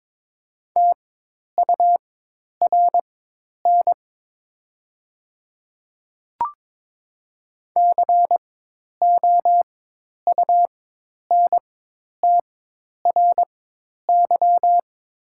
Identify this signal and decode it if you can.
Morse code, heard as a keyed tone, about 22 wpm, spaced as for 10 wpm, 700 Hz. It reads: TURN COUNTRY
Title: 4x Character Spacing - Top 300 Words - 22wpm